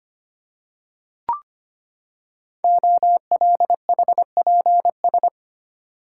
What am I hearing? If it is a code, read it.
Morse code, heard as a keyed tone, about 25 wpm, 700 Hz. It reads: OLHPS
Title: Code Groups - 5 Letters - 25wpm